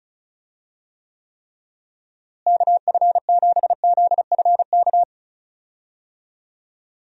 Morse code, heard as a keyed tone, about 34 wpm, 700 Hz. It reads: KF7ZFK
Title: USA Call Signs - 6 Characters - Repeated 3x - 35wpm